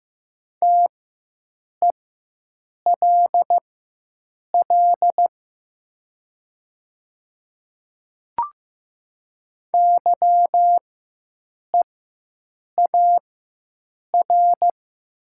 Morse code, heard as a keyed tone, about 15 wpm, spaced as for 7 wpm, 700 Hz. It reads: TELL YEAR
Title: 4x Character Spacing - Top 300 Words - 15wpm